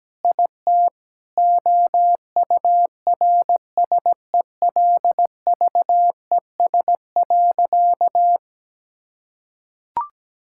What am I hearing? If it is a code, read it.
Morse code, heard as a keyed tone, about 17 wpm, 700 Hz. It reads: IT OURSELVES.
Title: Sentences from Top 1600 Words 17wpm